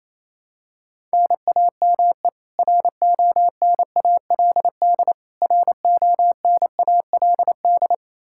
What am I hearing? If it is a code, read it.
Morse code, heard as a keyed tone, about 28 wpm, 700 Hz. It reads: NAME RONALD RONALD
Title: Practice QSO - Exchange 1 - Beginning - 28wpm